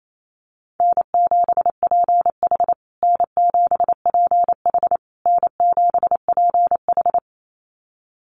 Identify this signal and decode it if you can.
Morse code, heard as a keyed tone, about 28 wpm, 700 Hz. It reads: N7PH N7PH N7PH
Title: USA Call Signs Repeated 3x 28wpm